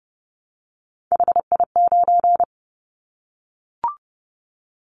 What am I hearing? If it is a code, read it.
Morse code, heard as a keyed tone, about 30 wpm, 700 Hz. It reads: HI9
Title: Callsign Prefixes for Top 100 DXCC - 30wpm